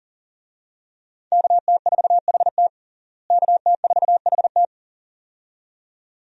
Morse code, heard as a keyed tone, about 40 wpm, 700 Hz. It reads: KT4HT KT4HT
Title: USA Call Signs - 5 Characters - Repeated 2x - 40wpm